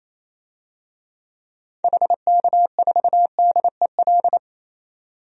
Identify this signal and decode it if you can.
Morse code, heard as a keyed tone, about 28 wpm, 700 Hz. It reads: HK4DEL
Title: DXCC-100 Call Signs - 6 Characters - 28wpm